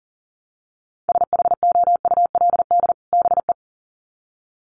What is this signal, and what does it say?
Morse code, heard as a keyed tone, about 40 wpm, 700 Hz. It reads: SHOULD BE
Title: Sets of 2 Words 40wpm